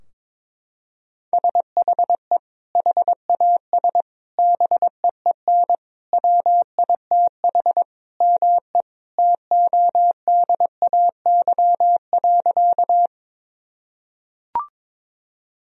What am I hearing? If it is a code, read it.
Morse code, heard as a keyed tone, about 22 wpm, 700 Hz. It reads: SHE HAS BEEN WITH ME TODAY.
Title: Sentences from Top 1000 Words - Review Part 4 - 22wpm